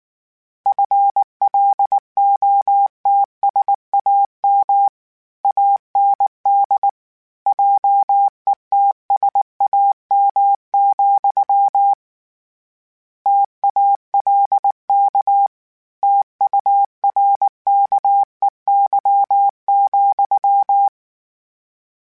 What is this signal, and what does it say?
Morse code, heard as a keyed tone, about 19 wpm, 800 Hz. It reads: FLOTSAM AND JETSAM, TALK TURKEY,
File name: Misc / Misc-mp3s / Word Pairs @18